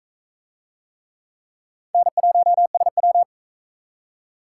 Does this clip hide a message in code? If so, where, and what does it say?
Yes: Morse code from 1.9 to 3.2 s, heard as a keyed tone, about 44 wpm, 700 Hz. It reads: N1SW